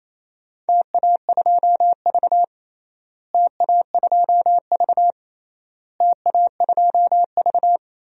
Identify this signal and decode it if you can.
Morse code, heard as a keyed tone, about 27 wpm, 700 Hz. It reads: TA2V TA2V TA2V
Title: DXCC-100 Call Signs - 4 Characters - Repeated 3x - 28wpm